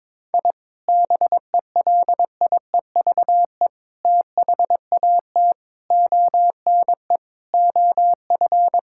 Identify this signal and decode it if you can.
Morse code, heard as a keyed tone, about 22 wpm, 700 Hz. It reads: I BELIEVE THAT ONE OF
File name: practice / 22wpm / Sentences with 4-Word Phrases - 22wpm